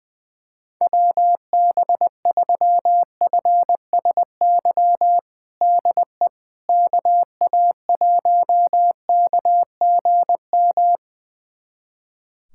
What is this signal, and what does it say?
Morse code, heard as a keyed tone, about 20 wpm, 700 Hz. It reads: WB3FSY DE KA1KGM
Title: USA Callsign DE Callsign - 6 Characters - 20wpm